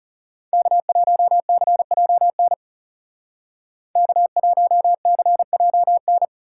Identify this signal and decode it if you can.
Morse code, heard as a keyed tone, about 39 wpm, 700 Hz. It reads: K1CJN K1CJN